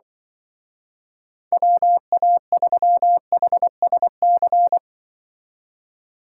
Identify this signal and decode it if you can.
Morse code, heard as a keyed tone, about 24 wpm, 700 Hz. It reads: WA3HSC
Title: Speed Racing - USA 2x3 Call Signs 20wpm